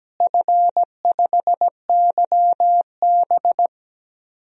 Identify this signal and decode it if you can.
Morse code, heard as a keyed tone, about 17 wpm, 700 Hz. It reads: F5YB
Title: DXCC-100 Call Signs - 4 Characters - Repeated 3x - 17wpm